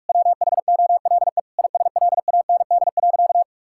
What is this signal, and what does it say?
Morse code, heard as a keyed tone, about 41 wpm, 700 Hz. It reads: WHOLE ISLAND.
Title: Sentences from Top 1000 Words - Review Part 5 - 45wpm